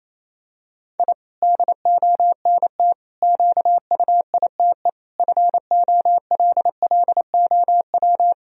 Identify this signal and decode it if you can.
Morse code, heard as a keyed tone, about 28 wpm, 700 Hz. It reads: I DONT QUITE FOLLOW
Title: Sentences from Top 1000 Words - Review Part 4 - 28wpm